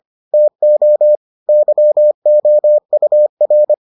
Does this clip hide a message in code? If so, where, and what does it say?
Yes: Morse code from 0.3 to 3.7 s, heard as a keyed tone, about 25 wpm, 600 Hz. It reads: TO YOUR